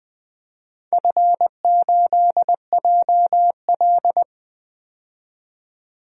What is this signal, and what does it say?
Morse code, heard as a keyed tone, about 20 wpm, 700 Hz. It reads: F8JL
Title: DXCC-100 Call Signs - 4 Characters - 20wpm